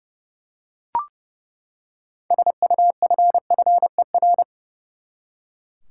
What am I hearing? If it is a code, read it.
Morse code, heard as a keyed tone, about 29 wpm, 700 Hz. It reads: SUFFER
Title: Six Letter Words - 30wpm